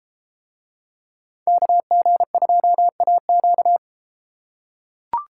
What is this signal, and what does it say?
Morse code, heard as a keyed tone, about 31 wpm, 700 Hz. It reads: KG2AQ